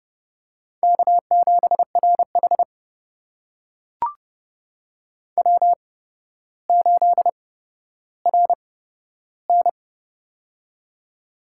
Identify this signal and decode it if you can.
Morse code, heard as a keyed tone, about 31 wpm, spaced as for 13 wpm, 700 Hz. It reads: K7RH W8RN
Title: USA Call Signs - 4 Characters - 8x Character Spacing - 30wpm